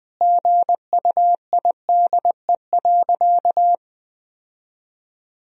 Morse code, heard as a keyed tone, about 20 wpm, 700 Hz. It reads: GUIDE.
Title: Sentences from Top 1000 Words - Review Part 3 - 20wpm